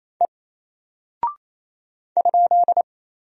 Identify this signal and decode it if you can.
Morse code, heard as a keyed tone, about 27 wpm, 700 Hz. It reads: E ?
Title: Single Character (Letter:Number:Punctuation:Prosign) 28wpm